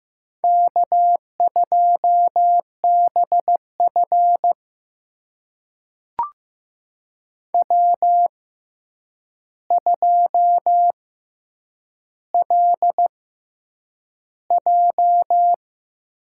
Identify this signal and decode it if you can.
Morse code, heard as a keyed tone, about 15 wpm, spaced as for 8 wpm, 700 Hz. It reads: K2BF W2LJ
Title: USA Call Signs - 4 Characters - 6x Character Spacing - 15wpm